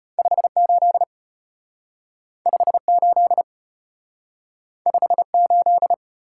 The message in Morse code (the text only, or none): 58 58 58